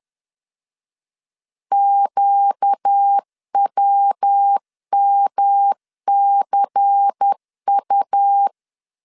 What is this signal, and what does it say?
Morse code, heard as a keyed tone, about 10 wpm, 800 Hz. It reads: QWMCU
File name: LEZIONI / 8 YJ45 / 8 YJ 10wpm 2651